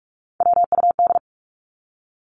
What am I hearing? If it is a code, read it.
Morse code, heard as a keyed tone, about 44 wpm, 700 Hz. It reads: WUD
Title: Abbreviations 45wpm